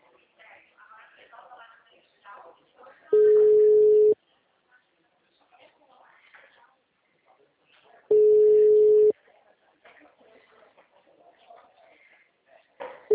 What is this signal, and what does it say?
Ringback tone (Europe, 425 Hz), with speech.